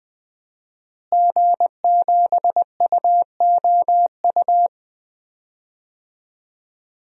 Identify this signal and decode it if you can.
Morse code, heard as a keyed tone, about 20 wpm, 700 Hz. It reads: G7UOU
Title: DXCC-100 Call Signs - 5 Characters - Repeated 3x - 20wpm